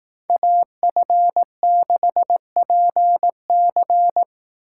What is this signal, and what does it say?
Morse code, heard as a keyed tone, about 18 wpm, 700 Hz. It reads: AF6PC
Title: Speed Racing - USA 2x2 Call Signs 15wpm